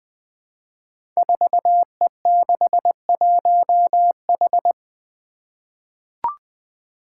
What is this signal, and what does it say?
Morse code, heard as a keyed tone, about 20 wpm, 700 Hz. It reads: 4E61H
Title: Five Challenging Letter-Number Combinations 20wpm